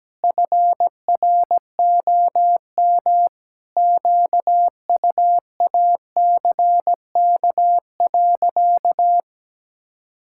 Morse code, heard as a keyed tone, about 17 wpm, 700 Hz. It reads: FROM QUACK.
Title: Pangram Challenge v2 - 17wpm